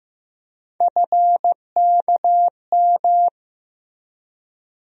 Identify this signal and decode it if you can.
Morse code, heard as a keyed tone, about 15 wpm, 700 Hz. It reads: FKM
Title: USA Call Sign Postfixes - 3 Letters 15wpm